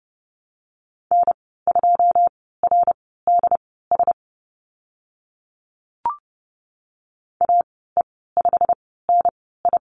Morse code, heard as a keyed tone, about 30 wpm, spaced as for 17 wpm, 700 Hz. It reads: N2RDS AE5NI
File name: Lesson 15g - Number 5 - Call Signs 30wpm